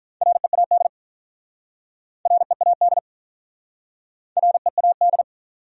Morse code, heard as a keyed tone, about 46 wpm, 700 Hz. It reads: READ READ READ